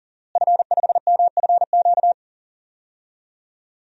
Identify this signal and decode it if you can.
Morse code, heard as a keyed tone, about 40 wpm, 700 Hz. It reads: F5MFQ
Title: DXCC-100 Call Signs - 5 Characters - Repeated 2x - 40wpm